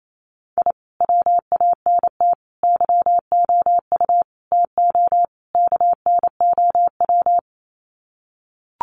Morse code, heard as a keyed tone, about 28 wpm, 700 Hz. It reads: I WANT YOU TO KNOW
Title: Sets of 5 Words Encore 28wpm